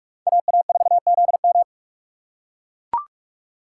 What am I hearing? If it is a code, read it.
Morse code, heard as a keyed tone, about 38 wpm, 700 Hz. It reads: AA4ZM